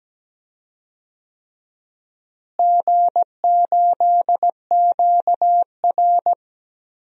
Morse code, heard as a keyed tone, about 17 wpm, 700 Hz. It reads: G8QR
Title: DXCC-100 Call Signs - 4 Characters - Repeated 3x - 17wpm